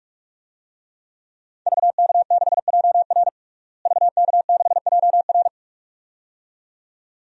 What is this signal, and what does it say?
Morse code, heard as a keyed tone, about 43 wpm, 700 Hz. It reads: VK6JR VK6JR
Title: DXCC-100 Call Signs - 5 Characters - Repeated 2x - 45wpm